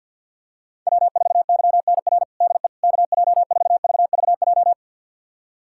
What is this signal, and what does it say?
Morse code, heard as a keyed tone, about 48 wpm, 700 Hz. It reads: W4YNR DE KJ4VVJ
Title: USA Callsign DE Callsign 50wpm